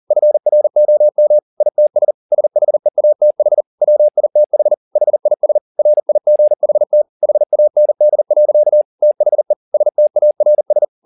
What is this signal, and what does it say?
Morse code, heard as a keyed tone, about 40 wpm, 600 Hz. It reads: FROM ITS SHEATH WITH HIS RIGHT HAND. THE STARS
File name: the old man and the sea 2 40wpm